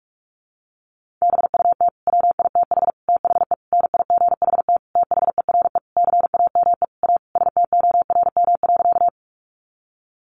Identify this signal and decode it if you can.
Morse code, heard as a keyed tone, about 44 wpm, 700 Hz. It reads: BUT WITH THE NIGHT THERE CAME A STORM.
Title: Sentences from Top 1500 Words 45wpm